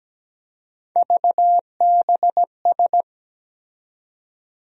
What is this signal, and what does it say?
Morse code, heard as a keyed tone, about 17 wpm, 700 Hz. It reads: VBS
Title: USA Call Sign Postfixes - 3 Letters 17wpm